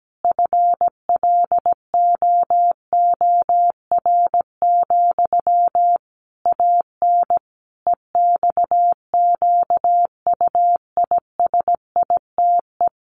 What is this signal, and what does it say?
Morse code, heard as a keyed tone, about 17 wpm, 700 Hz. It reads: FLOOR, AN EXQUISITE
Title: A Sound Of Thunder 17wpm